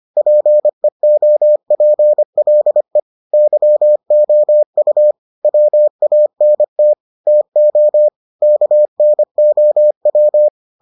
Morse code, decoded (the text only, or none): PEOPLE YOU WANT TO KNOW